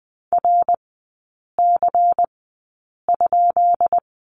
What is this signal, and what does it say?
Morse code, heard as a keyed tone, about 19 wpm, spaced as for 8 wpm, 700 Hz. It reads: RC?
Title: 133 Lesson 22e - Question Mark - Three Characters 20wpm